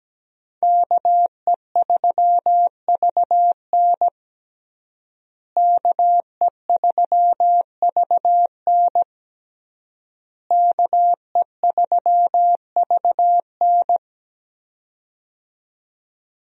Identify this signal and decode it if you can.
Morse code, heard as a keyed tone, about 17 wpm, 700 Hz. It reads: KE3VN KE3VN KE3VN
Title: USA Call Signs - 5 Characters - Repeated 3x - 17wpm